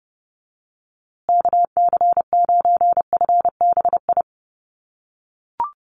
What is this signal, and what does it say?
Morse code, heard as a keyed tone, about 30 wpm, 700 Hz. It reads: KC9FBI
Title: USA Call Signs Repeated 2x 30wpm